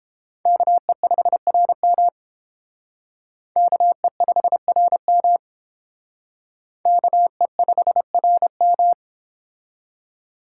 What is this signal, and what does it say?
Morse code, heard as a keyed tone, about 30 wpm, 700 Hz. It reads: KE5RM KE5RM KE5RM